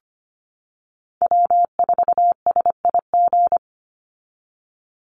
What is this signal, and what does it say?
Morse code, heard as a keyed tone, about 25 wpm, 700 Hz. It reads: W4SIG
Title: Call Signs - 25wpm